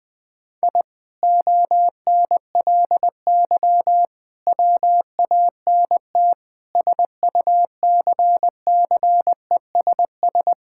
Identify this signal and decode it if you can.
Morse code, heard as a keyed tone, about 19 wpm, 700 Hz. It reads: I ONLY WANT SUCCESS
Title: Sentences from Top 1000 Words - Review Part 6 - 20wpm